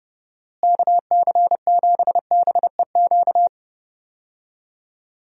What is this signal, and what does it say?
Morse code, heard as a keyed tone, about 30 wpm, 700 Hz. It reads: KC7BEQ
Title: USA Call Signs - 6 Characters - Repeated 2x - 30wpm